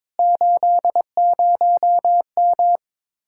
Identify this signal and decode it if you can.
Morse code, heard as a keyed tone, about 22 wpm, 700 Hz. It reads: 80M
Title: Speed-Racing Common QSO Phrases - 22wpm